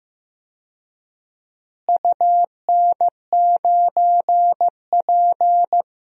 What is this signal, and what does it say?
Morse code, heard as a keyed tone, about 15 wpm, 700 Hz. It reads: UN9P
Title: DXCC-100 Call Signs - 4 Characters - Repeated 2x - 15wpm